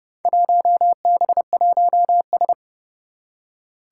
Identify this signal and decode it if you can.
Morse code, heard as a keyed tone, about 30 wpm, 700 Hz. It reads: 1B1S
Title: Four Challenging Letter-Number Combinations 30wpm